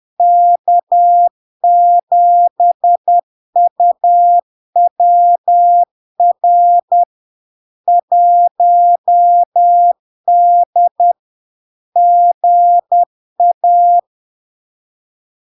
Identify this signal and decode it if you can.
Morse code, heard as a keyed tone, about 10 wpm, 700 Hz. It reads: K7UWR 1D GA
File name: LARK-FD-2022-Log-in-CW-10wpm-700-Hz